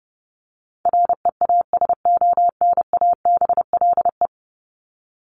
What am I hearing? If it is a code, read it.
Morse code, heard as a keyed tone, about 30 wpm, 700 Hz. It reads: REASONABLE
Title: Top 1501-1750 Words 30wpm